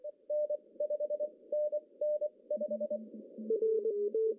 Morse code, heard as a keyed tone, about 24 wpm, spaced as for 19 wpm, 600 Hz. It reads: N 5NN 5
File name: CW / cqww-cw-k5tr